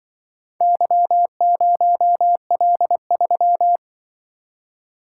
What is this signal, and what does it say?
Morse code, heard as a keyed tone, about 24 wpm, 700 Hz. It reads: Y0L3
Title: Speed-Racing - Four Letter-Number Combinations - 20wpm